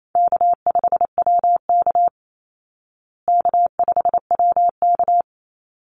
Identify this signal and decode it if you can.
Morse code, heard as a keyed tone, about 28 wpm, 700 Hz. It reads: K5WK K5WK
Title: Easy Call Signs Repeated 2x - 28wpm